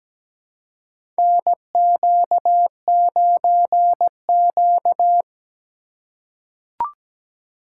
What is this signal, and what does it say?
Morse code, heard as a keyed tone, about 17 wpm, 700 Hz. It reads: NQ9Q
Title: USA Call Signs - 4 Characters - 2x Character Spacing - 17wpm